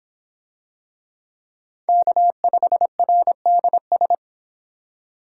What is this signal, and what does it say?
Morse code, heard as a keyed tone, about 25 wpm, 700 Hz. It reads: K5RDS